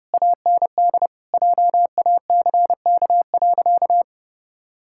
Morse code, heard as a keyed tone, about 30 wpm, 700 Hz. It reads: AND JACK.
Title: Pangram Challenge v2 - 30wpm